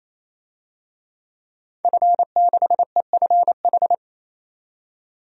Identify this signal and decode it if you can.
Morse code, heard as a keyed tone, about 28 wpm, 700 Hz. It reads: F6EFH
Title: DXCC-100 Call Signs - 5 Characters - 28wpm